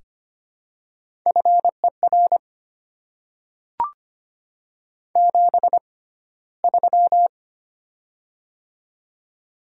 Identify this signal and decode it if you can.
Morse code, heard as a keyed tone, about 25 wpm, spaced as for 15 wpm, 700 Hz. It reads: FER 73
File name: Top 90 QSO Elements - 6x Character Spacing - 25wpm